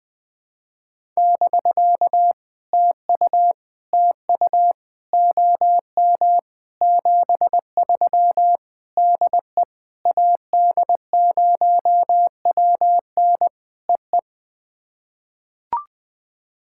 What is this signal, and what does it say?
Morse code, heard as a keyed tone, about 20 wpm, 700 Hz. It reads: <BK> TU TU OM 73 DE AD0WN EE
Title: POTA QSO Practice - 20wpm